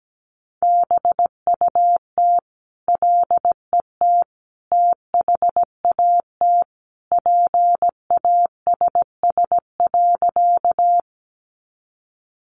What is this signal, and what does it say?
Morse code, heard as a keyed tone, about 17 wpm, 700 Hz. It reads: BUT LET THAT PASS.
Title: Sentences from Top 400 Words 17wpm